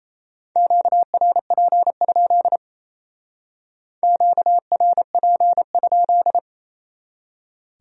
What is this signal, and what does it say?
Morse code, heard as a keyed tone, about 33 wpm, 700 Hz. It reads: QRP? QRP?
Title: Speed-Racing Q-Codes - 28wpm